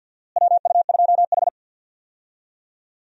Morse code, heard as a keyed tone, about 52 wpm, 700 Hz. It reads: WU2H
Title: USA Call Signs - 4 Characters - Repeated 3x - 50wpm